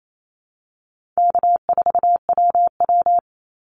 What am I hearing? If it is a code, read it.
Morse code, heard as a keyed tone, about 28 wpm, 700 Hz. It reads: K4WW
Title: Easy Call Signs Repeated 3x - 28wpm